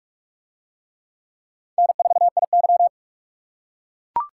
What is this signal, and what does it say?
Morse code, heard as a keyed tone, about 44 wpm, 700 Hz. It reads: N4IY